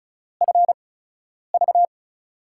Morse code, heard as a keyed tone, about 36 wpm, spaced as for 10 wpm, 700 Hz. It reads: FV
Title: USA Call Signs - 5 Characters - 8x Character Spacing - 35wpm